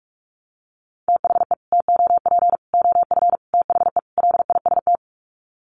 Morse code, heard as a keyed tone, about 44 wpm, 700 Hz. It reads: THE TOP OF THE LIST
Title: Sets of 5 Words Encore2 45wpm